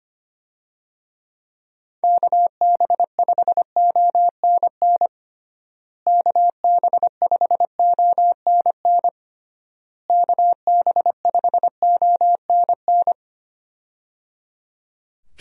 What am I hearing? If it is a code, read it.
Morse code, heard as a keyed tone, about 25 wpm, 700 Hz. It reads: KB5ONN KB5ONN KB5ONN
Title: USA Call Signs - 6 Characters - Repeated 3x - 25wpm